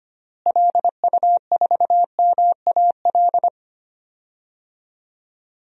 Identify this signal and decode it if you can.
Morse code, heard as a keyed tone, about 25 wpm, 700 Hz. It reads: LU4MAL
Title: DXCC-100 Call Signs - 6 Characters - Repeated 3x - 25wpm